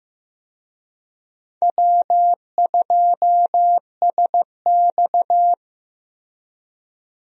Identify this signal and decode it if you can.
Morse code, heard as a keyed tone, about 15 wpm, 700 Hz. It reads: W2SX
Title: USA Call Signs - 4 Characters - 15wpm